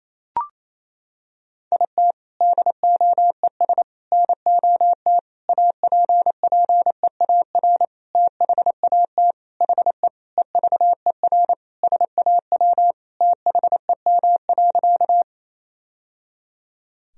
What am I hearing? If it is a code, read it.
Morse code, heard as a keyed tone, about 28 wpm, 700 Hz. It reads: IT DOES NOT APPEAR THAT HE EVER SAW THEM.
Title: Sentences from Top 1000 Words - Review Part 2 - 28wpm